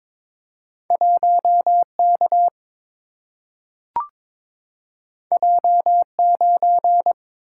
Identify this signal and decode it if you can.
Morse code, heard as a keyed tone, about 22 wpm, 700 Hz. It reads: 1K J9
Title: Single-Letter Single-Number 22wpm